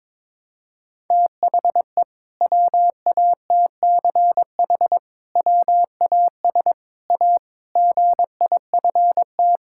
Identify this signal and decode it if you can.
Morse code, heard as a keyed tone, about 22 wpm, 700 Hz. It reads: THE WATCH WAS A GIFT
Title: Sentences from Top 1000 Words - Review Part 6 - 22wpm